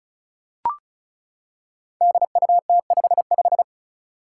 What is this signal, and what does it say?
Morse code, heard as a keyed tone, about 36 wpm, 700 Hz. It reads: DUT55
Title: Five Challenging Letter-Number Combinations 35wpm